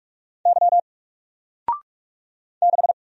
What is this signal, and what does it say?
Morse code, heard as a keyed tone, about 46 wpm, 700 Hz. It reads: Y 6
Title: Single Letter-Number 45wpm